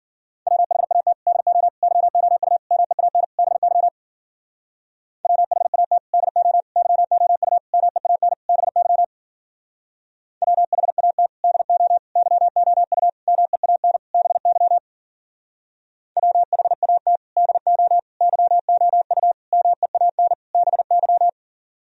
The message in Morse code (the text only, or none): WHAT DO YOU MEAN BY WHAT DO YOU MEAN BY WHAT DO YOU MEAN BY WHAT DO YOU MEAN BY